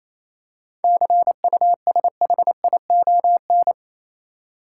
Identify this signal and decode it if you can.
Morse code, heard as a keyed tone, about 26 wpm, 700 Hz. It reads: CUSHION
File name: Easy Words - Speed-Racing with Hints - 22wpm